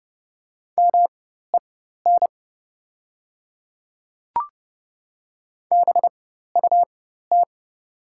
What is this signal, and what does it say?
Morse code, heard as a keyed tone, about 30 wpm, spaced as for 14 wpm, 700 Hz. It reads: MEN BUT
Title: 4x Character Spacing - Top 200 Words - 30wpm